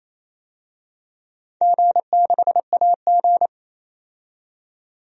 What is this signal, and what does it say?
Morse code, heard as a keyed tone, about 28 wpm, 700 Hz. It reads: G6AG
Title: DXCC-100 Call Signs - 4 Characters - 28wpm